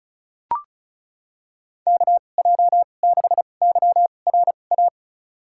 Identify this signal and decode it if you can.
Morse code, heard as a keyed tone, about 34 wpm, spaced as for 25 wpm, 700 Hz. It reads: KJ6YRA
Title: USA Call Signs - 2x Character Spacing - 35wpm